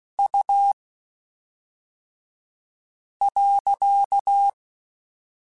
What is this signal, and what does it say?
Morse code, heard as a keyed tone, about 16 wpm, spaced as for 3 wpm, 800 Hz. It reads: U.
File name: (spaced) / QSOs Spaced-mp3s / WA7QOX 1 @5 sp